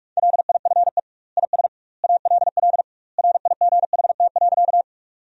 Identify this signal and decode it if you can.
Morse code, heard as a keyed tone, about 46 wpm, 700 Hz. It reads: LIFE IS ALL RIGHT.
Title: 2x Word-Spacing - Sentences from Top 300 Words - 45wpm